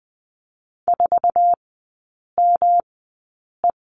Morse code, heard as a keyed tone, about 20 wpm, spaced as for 8 wpm, 700 Hz. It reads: 4ME
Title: 127 Lesson 21e - Number 6 - Three Characters 20wpm